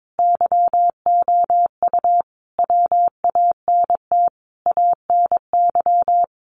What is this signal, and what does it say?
Morse code, heard as a keyed tone, about 22 wpm, 700 Hz. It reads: YOU WANT ANY
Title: Sentences from Top 400 Words 22wpm